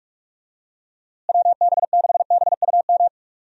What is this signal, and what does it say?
Morse code, heard as a keyed tone, about 41 wpm, 700 Hz. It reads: WB6BUM
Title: USA Call Signs - 6 Characters - Repeated 2x - 45wpm